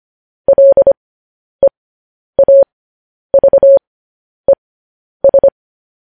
Morse code, heard as a keyed tone, about 25 wpm, spaced as for 10 wpm, 550 Hz. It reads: LEAVES
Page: news headlines in morse code at 10 wpm